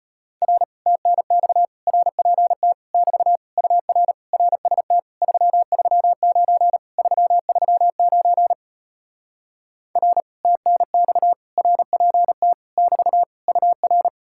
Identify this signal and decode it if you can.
Morse code, heard as a keyed tone, about 33 wpm, 700 Hz. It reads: R TNX RPT <BT> UR RST 339 339 R TNX RPT <BT> UR